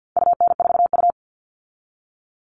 Morse code, heard as a keyed tone, about 50 wpm, 700 Hz. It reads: UN4U
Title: All Cut Numbers - 4-Digit 50wpm